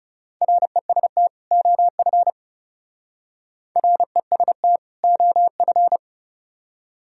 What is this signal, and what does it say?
Morse code, heard as a keyed tone, about 38 wpm, 700 Hz. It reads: REST OF REST OF